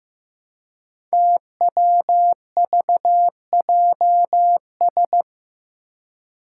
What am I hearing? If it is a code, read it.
Morse code, heard as a keyed tone, about 15 wpm, 700 Hz. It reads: TWVJS
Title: Five Challenging Letter-Number Combinations 15wpm